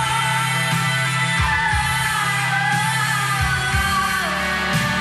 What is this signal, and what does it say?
Emergency Alert System (SAME) tones, heard as harsh buzzing bursts. It is heard from 0.9 to 3.8 s.